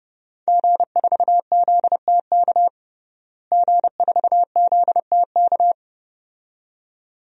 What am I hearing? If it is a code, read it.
Morse code, heard as a keyed tone, about 30 wpm, 700 Hz. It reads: G4ZTK G4ZTK
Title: DXCC-100 Call Signs - 5 Characters - Repeated 2x - 30wpm